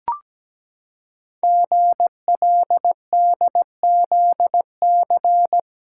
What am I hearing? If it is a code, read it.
Morse code, heard as a keyed tone, about 17 wpm, 700 Hz. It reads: GLDZC
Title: Code Groups - 5 Letters - 17wpm